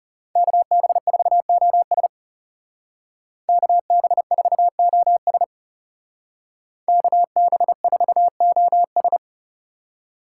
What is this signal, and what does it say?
Morse code, heard as a keyed tone, about 37 wpm, 700 Hz. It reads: KB4OS KB4OS KB4OS
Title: Speed Racing - USA 2x2 Call Signs 30wpm